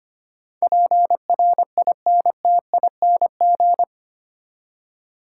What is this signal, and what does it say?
Morse code, heard as a keyed tone, about 25 wpm, 700 Hz. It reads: PRINTING
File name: 8 Letter Words - First 3 Letters Given - 25wpm